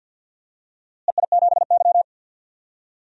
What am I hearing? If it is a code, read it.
Morse code, heard as a keyed tone, about 49 wpm, 700 Hz. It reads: EI7Y
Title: DXCC-100 Call Signs - 4 Characters - Repeated 2x - 50wpm